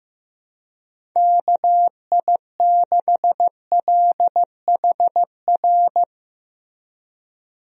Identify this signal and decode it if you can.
Morse code, heard as a keyed tone, about 15 wpm, 700 Hz. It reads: KI6LHR
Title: USA Call Signs - 6 Characters - Repeated 3x - 15wpm